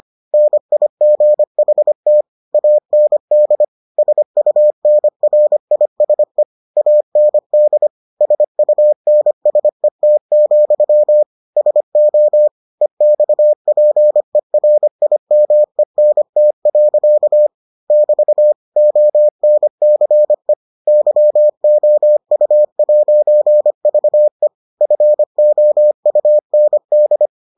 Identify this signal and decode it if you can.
Morse code, heard as a keyed tone, about 25 wpm, 600 Hz. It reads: NIGHT AND SUNRISE AND SUNSET, SO EXPERIMENT. <BT> ONCE YOU'VE FOUND